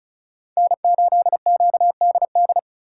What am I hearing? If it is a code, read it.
Morse code, heard as a keyed tone, about 36 wpm, 700 Hz. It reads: N8QDD
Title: USA Call Signs - 5 Characters - 4x Character Spacing - 35wpm